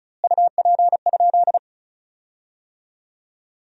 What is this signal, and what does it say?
Morse code, heard as a keyed tone, about 36 wpm, 700 Hz. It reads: UP?
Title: 2x Word-Spacing - Sentences from Top 500 Words - 35wpm